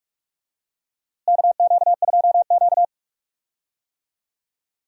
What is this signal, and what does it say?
Morse code, heard as a keyed tone, about 46 wpm, 700 Hz. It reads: KQ2Q
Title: USA Call Signs - 4 Characters - Repeated 2x - 45wpm